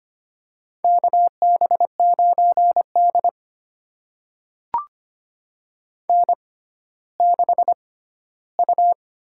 Morse code, heard as a keyed tone, about 25 wpm, spaced as for 14 wpm, 700 Hz. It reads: KB9D N6U
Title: USA Call Signs - 4 Characters - 6x Character Spacing - 25wpm